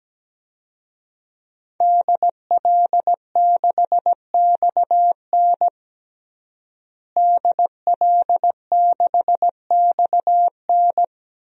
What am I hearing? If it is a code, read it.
Morse code, heard as a keyed tone, about 17 wpm, 700 Hz. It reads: DL6XN DL6XN
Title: DXCC-100 Call Signs - 5 Characters - Repeated 2x - 17wpm